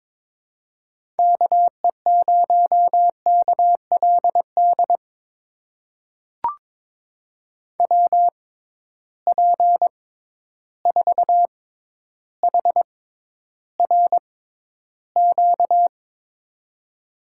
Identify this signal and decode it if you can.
Morse code, heard as a keyed tone, about 22 wpm, spaced as for 11 wpm, 700 Hz. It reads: KE0KLD WP4HRQ
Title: USA Call Signs - 6 Characters - 6x Character Spacing - 22wpm